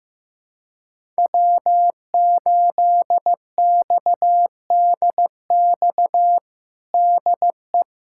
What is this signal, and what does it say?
Morse code, heard as a keyed tone, about 15 wpm, 700 Hz. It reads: W8XDX DE
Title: USA Callsign DE Callsign - 5 Characters - 15wpm